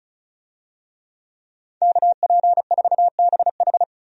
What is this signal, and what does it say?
Morse code, heard as a keyed tone, about 34 wpm, 700 Hz. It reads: KP4BH
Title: DXCC-100 Call Signs - 5 Characters - 35wpm